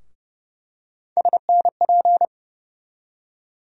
Morse code, heard as a keyed tone, about 30 wpm, 700 Hz. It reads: SNP
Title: USA Call Sign Postfixes - 3 Letters 30wpm